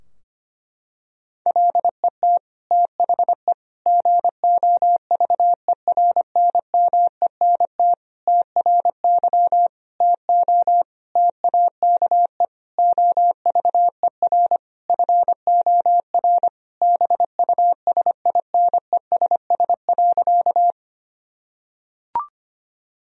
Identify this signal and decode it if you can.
Morse code, heard as a keyed tone, about 25 wpm, 700 Hz. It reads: LET THE GOVERNMENT TRY TO TAKE OVER FOR BUSINESS.
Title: Sentences from Top 1000 Words - Review Part 3 - 25wpm